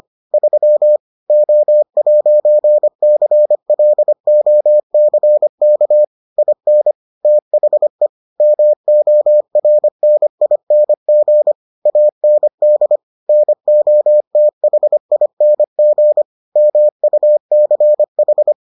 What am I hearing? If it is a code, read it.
Morse code, heard as a keyed tone, about 25 wpm, 600 Hz. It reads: 3 O'CLOCK IN THE MORNING AND NOTHING MUCH